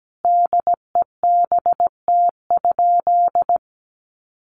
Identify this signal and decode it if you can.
Morse code, heard as a keyed tone, about 17 wpm, 700 Hz. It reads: DEBT?
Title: Sentences from Top 1300 Words 17wpm